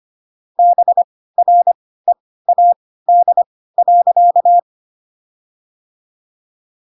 Morse code, heard as a keyed tone, about 25 wpm, spaced as for 16 wpm, 700 Hz. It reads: BREAD.